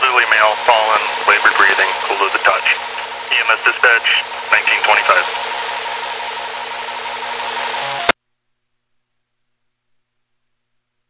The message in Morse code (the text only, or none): none